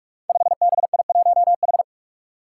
Morse code, heard as a keyed tone, about 44 wpm, 700 Hz. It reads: 5BI1H
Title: Five Challenging Letter-Number Combinations 45wpm